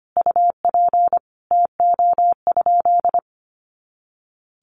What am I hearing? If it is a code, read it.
Morse code, heard as a keyed tone, about 25 wpm, 700 Hz. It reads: UP TO?
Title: Sentences from Top 500 Words 25wpm